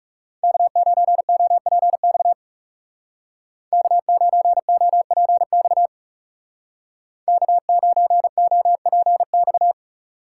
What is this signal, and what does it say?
Morse code, heard as a keyed tone, about 40 wpm, 700 Hz. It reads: K9OPX K9OPX K9OPX